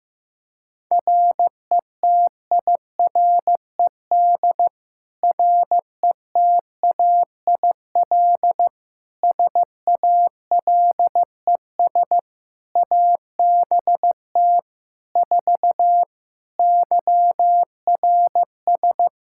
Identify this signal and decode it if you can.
Morse code, heard as a keyed tone, about 15 wpm, 700 Hz. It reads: RETIRED RETAIL SALES ABT 4 YRS
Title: Practice QSO - Exchange 3 - More Info - 15wpm